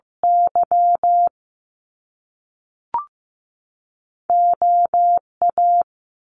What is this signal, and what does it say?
Morse code, heard as a keyed tone, about 15 wpm, spaced as for 3 wpm, 700 Hz. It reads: YO A
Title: Top Two Letter Combinations in Words 15wpm